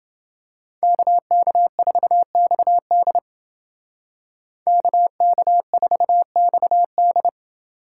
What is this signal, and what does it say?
Morse code, heard as a keyed tone, about 28 wpm, 700 Hz. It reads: KK4XD KK4XD